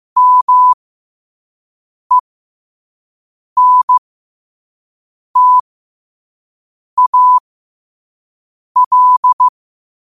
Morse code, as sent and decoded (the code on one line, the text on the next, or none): -- . -. - .- .-..
MENTAL